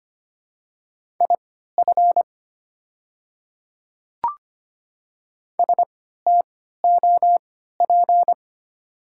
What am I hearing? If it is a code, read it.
Morse code, heard as a keyed tone, about 25 wpm, spaced as for 14 wpm, 700 Hz. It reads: IF STOP
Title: 3x Character Spacing - Top 500 Words - 25wpm